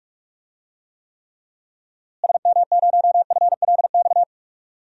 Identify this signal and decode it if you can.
Morse code, heard as a keyed tone, about 42 wpm, 700 Hz. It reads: SM0FLX
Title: DXCC-100 Call Signs - 6 Characters - Repeated 3x - 45wpm